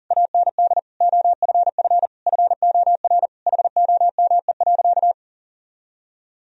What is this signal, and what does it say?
Morse code, heard as a keyed tone, about 40 wpm, 700 Hz. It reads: AND OFF FOR HOME.
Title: Sentences from Top 1000 Words - Review Part 4 - 40wpm